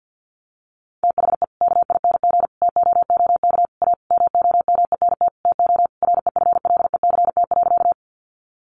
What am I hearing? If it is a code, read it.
Morse code, heard as a keyed tone, about 50 wpm, 700 Hz. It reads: THE KING TOOK A MOMENT TO REFLECT.
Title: Sentences from Top 1500 Words 50wpm